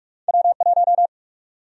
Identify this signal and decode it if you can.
Morse code, heard as a keyed tone, about 43 wpm, 700 Hz. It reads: W1
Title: USA Call Sign Prefixes - Repeated 2x - 45wpm